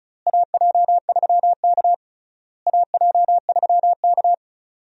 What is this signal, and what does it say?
Morse code, heard as a keyed tone, about 36 wpm, 700 Hz. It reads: AJ3K AJ3K
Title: USA Call Signs - 4 Characters - Repeated 3x - 35wpm